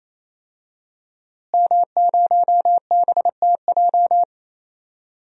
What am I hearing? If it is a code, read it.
Morse code, heard as a keyed tone, about 28 wpm, 700 Hz. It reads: M0BTJ
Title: DXCC-100 Call Signs - 5 Characters - Repeated 3x - 28wpm